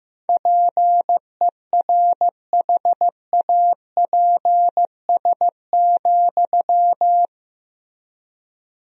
Morse code, heard as a keyed tone, about 15 wpm, 700 Hz. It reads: PERHAPS,
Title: 3x Word-Spacing - Sentences from Top 500 Words - 15wpm